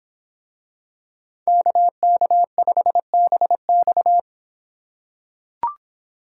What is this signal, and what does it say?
Morse code, heard as a keyed tone, about 26 wpm, 700 Hz. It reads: KK5BX